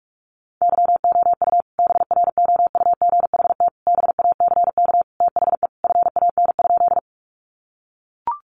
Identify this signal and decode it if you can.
Morse code, heard as a keyed tone, about 44 wpm, 700 Hz. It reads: YOU BROUGHT BACK THE FAN?
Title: Sentences from Top 1300 Words 45wpm